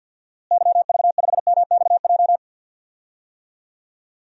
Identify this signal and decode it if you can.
Morse code, heard as a keyed tone, about 52 wpm, 700 Hz. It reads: YV5MXJ
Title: DXCC-100 Call Signs - 6 Characters - Repeated 2x - 50wpm